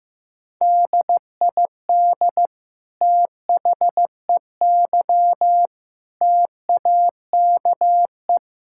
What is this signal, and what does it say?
Morse code, heard as a keyed tone, about 15 wpm, 700 Hz. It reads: DID THEY TAKE
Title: Sentences from Top 1000 Words - Review Part 6 - 15wpm